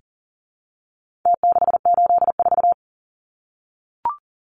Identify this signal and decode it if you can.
Morse code, heard as a keyed tone, about 40 wpm, 700 Hz. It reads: T684
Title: Numbers - Contest Style - 4-Digit 40wpm